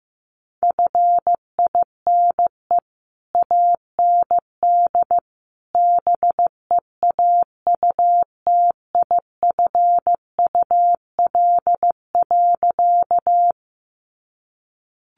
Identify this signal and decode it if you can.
Morse code, heard as a keyed tone, about 15 wpm, 700 Hz. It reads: FINE AND BEAUTIFUL.
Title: Sentences from Top 900 Words 15wpm